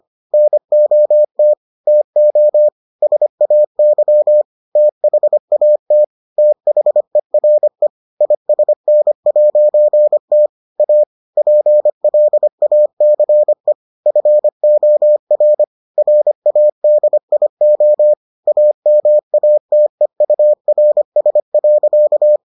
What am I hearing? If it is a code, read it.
Morse code, heard as a keyed tone, about 25 wpm, 600 Hz. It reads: NOT TO SAY THAT THERE ISN'T A PLACE FOR RADIO AMATEURS.